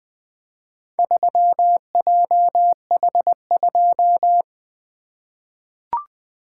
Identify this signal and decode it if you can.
Morse code, heard as a keyed tone, about 20 wpm, 700 Hz. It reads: 3JH2
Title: Four Letter-Number Combinations - 20wpm